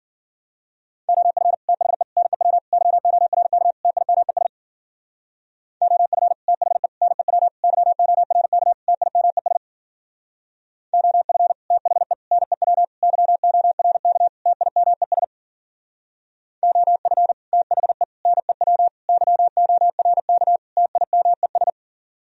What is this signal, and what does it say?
Morse code, heard as a keyed tone, about 46 wpm, 700 Hz. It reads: OF THE NEW YORK TIMES OF THE NEW YORK TIMES OF THE NEW YORK TIMES OF THE NEW YORK TIMES
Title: Speed-Racing - Sets of 5 Words 40wpm